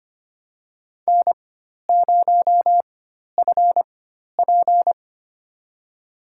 Morse code, heard as a keyed tone, about 25 wpm, spaced as for 12 wpm, 700 Hz. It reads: N0FP
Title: USA Call Signs - 4 Characters - 4x Character Spacing - 25wpm